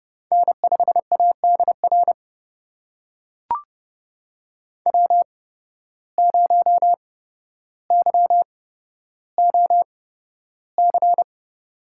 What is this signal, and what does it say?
Morse code, heard as a keyed tone, about 29 wpm, spaced as for 13 wpm, 700 Hz. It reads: N5ADR W0YOC
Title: USA Call Signs - 5 Characters - 8x Character Spacing - 30wpm